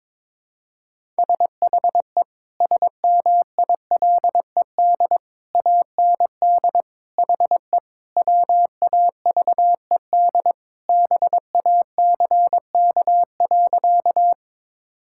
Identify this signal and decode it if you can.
Morse code, heard as a keyed tone, about 22 wpm, 700 Hz. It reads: SHE SMILED AND HE WAVED BACK.
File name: practice / 22wpm / Sentences with 2-Word Phrases - 22wpm